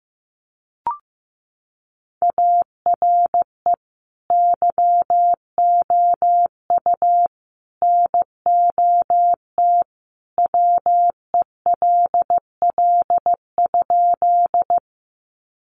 Morse code, heard as a keyed tone, about 15 wpm, 700 Hz. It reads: ARE YOU NOT WELL?
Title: Sentences from Top 600 Words 15wpm